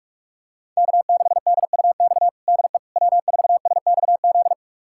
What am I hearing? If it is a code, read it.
Morse code, heard as a keyed tone, about 46 wpm, 700 Hz. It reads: K6DUX DE W4SXZ
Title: USA Callsign DE Callsign - 5 Characters - 45wpm